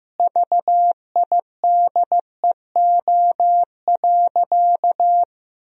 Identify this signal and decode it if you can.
Morse code, heard as a keyed tone, about 15 wpm, 700 Hz. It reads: VIDEO.
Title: Sentences from Top 1000 Words - Review Part 4 - 15wpm